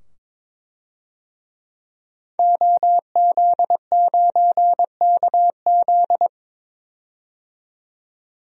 Morse code, heard as a keyed tone, about 22 wpm, 700 Hz. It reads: OZ9KZ
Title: DXCC-100 Call Signs - 5 Characters - 22wpm